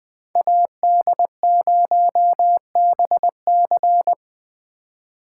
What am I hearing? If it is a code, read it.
Morse code, heard as a keyed tone, about 20 wpm, 700 Hz. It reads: AD0BC
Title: 2023 POTA Hunter Call Signs - 20wpm